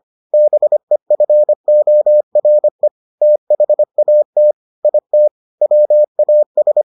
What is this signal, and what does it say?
Morse code, heard as a keyed tone, about 25 wpm, 600 Hz. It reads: BEFORE THAT IT WAS